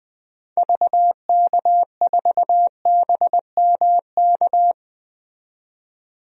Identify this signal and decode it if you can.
Morse code, heard as a keyed tone, about 21 wpm, 700 Hz. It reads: VK4BMK
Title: DXCC-100 Call Signs - 6 Characters - 20wpm